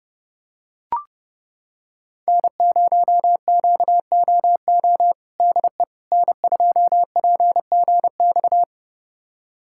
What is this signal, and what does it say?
Morse code, heard as a keyed tone, about 31 wpm, 700 Hz. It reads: N0QOO DE N2PGX
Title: USA Callsign DE Callsign - 5 Characters - 30wpm